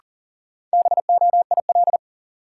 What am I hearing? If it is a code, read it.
Morse code, heard as a keyed tone, about 40 wpm, 700 Hz. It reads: BOIL
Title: Sets of 5 Words Encore - New Word Drill - 40wpm